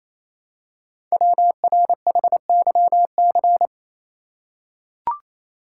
Morse code, heard as a keyed tone, about 28 wpm, 700 Hz. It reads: WRHYC
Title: Code Groups - 5 Letters - 28wpm